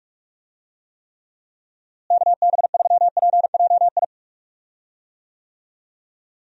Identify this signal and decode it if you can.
Morse code, heard as a keyed tone, about 46 wpm, 700 Hz. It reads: KB3PJI
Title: USA Call Signs - 6 Characters - Repeated 2x - 45wpm